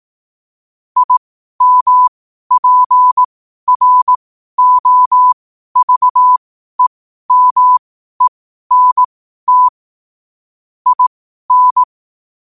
Morse code, heard as a keyed tone, about 18 wpm, spaced as for 12 wpm, 1000 Hz. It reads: IMPROVEMENT IN
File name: qotdfiles / 12wpm